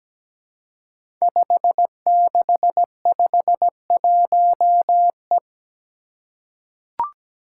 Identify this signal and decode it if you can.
Morse code, heard as a keyed tone, about 17 wpm, 700 Hz. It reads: 5651E
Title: Five Challenging Letter-Number Combinations 17wpm